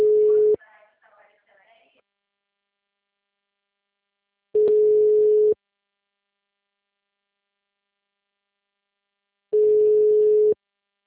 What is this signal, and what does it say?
Ringback tone (Europe, 425 Hz).